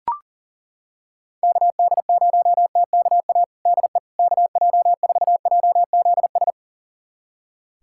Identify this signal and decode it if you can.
Morse code, heard as a keyed tone, about 40 wpm, 700 Hz. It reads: KD0TKA DE KJ4JZS
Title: USA Callsign DE Callsign - 6 Characters - 40wpm